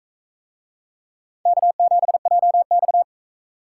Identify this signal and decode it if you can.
Morse code, heard as a keyed tone, about 41 wpm, 700 Hz. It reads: K7JX